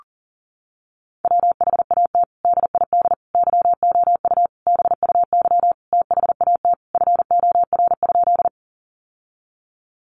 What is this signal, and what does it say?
Morse code, heard as a keyed tone, about 40 wpm, 700 Hz. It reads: WHAT DID YOU BUY THAT FOR?
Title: Sentences from Top 700 Words 40wpm